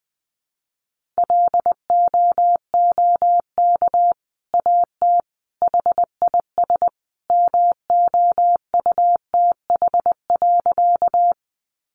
Morse code, heard as a keyed tone, about 20 wpm, 700 Hz. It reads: LOOK AT HIS MOUTH.
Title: Sentences from Top 1500 Words 20wpm